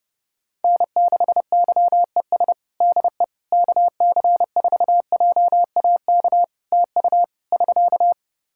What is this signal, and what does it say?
Morse code, heard as a keyed tone, about 30 wpm, 700 Hz. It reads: N6YES DE KC4JAK TU <SK>
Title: K1USN SST QSO Practice 30wpm